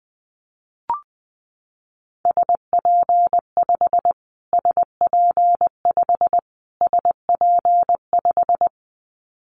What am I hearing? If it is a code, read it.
Morse code, heard as a keyed tone, about 20 wpm, 700 Hz. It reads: SP5 SP5 SP5
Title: Callsign Prefixes for Top 25 DXCC - Repeated 3x - 20wpm